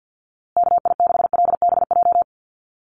Morse code, heard as a keyed tone, about 50 wpm, 700 Hz. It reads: KI6LBJ
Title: USA Call Signs Repeated 2x 50wpm